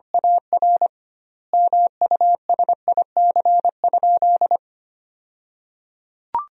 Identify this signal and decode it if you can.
Morse code, heard as a keyed tone, about 25 wpm, 700 Hz. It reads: AR MUSIC?
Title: 2x Word-Spacing - Sentences from Top 300 Words - 25wpm